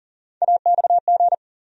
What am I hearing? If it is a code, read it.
Morse code, heard as a keyed tone, about 40 wpm, 700 Hz. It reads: AXG
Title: USA Call Sign Postfixes - 3 Letters 40wpm